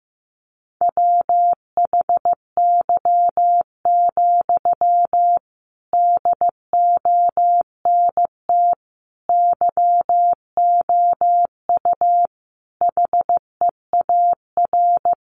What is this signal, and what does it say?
Morse code, heard as a keyed tone, about 15 wpm, 700 Hz. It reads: WHY, DONT YOU HEAR
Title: Sentences from Top 300 Words 15wpm